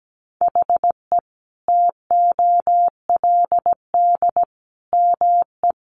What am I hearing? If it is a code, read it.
Morse code, heard as a keyed tone, about 17 wpm, 700 Hz. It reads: HE TOLD ME
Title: Sentences from Top 1500 Words 17wpm